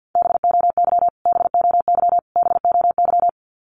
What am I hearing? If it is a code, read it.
Morse code, heard as a keyed tone, about 50 wpm, 700 Hz. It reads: BOY BOY BOY
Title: Top 500 Words Repeated 50wpm